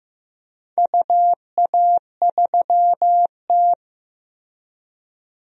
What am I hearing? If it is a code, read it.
Morse code, heard as a keyed tone, about 15 wpm, 700 Hz. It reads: UA3T
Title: DXCC-100 Call Signs - 4 Characters - Repeated 3x - 15wpm